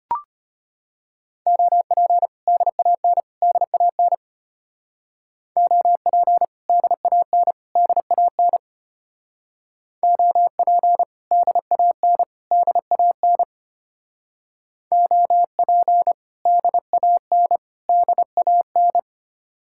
Morse code, heard as a keyed tone, about 32 wpm, 700 Hz. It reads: OP DAN DAN OP DAN DAN OP DAN DAN OP DAN DAN